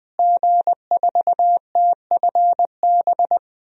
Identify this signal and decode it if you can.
Morse code, heard as a keyed tone, about 20 wpm, 700 Hz. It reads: G4TFB
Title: DXCC-100 Call Signs - 5 Characters - Repeated 3x - 20wpm